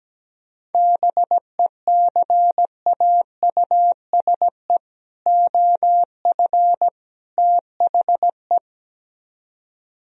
Morse code, heard as a keyed tone, about 17 wpm, 700 Hz. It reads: BECAUSE OF THE
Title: Speed-Racing - Sets of 3 Words 17wpm